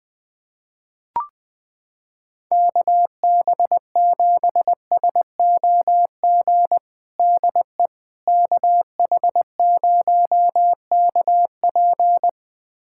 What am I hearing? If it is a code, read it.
Morse code, heard as a keyed tone, about 20 wpm, 700 Hz. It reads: KB7SOG DE KH0KP
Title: USA Callsign DE Callsign 20wpm